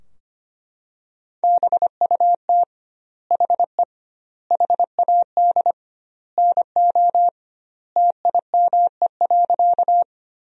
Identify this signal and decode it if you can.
Morse code, heard as a keyed tone, about 25 wpm, 700 Hz. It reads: BUT HE HAD NO TIME.
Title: 2x Word-Spacing - Sentences from Top 100 Words - 25wpm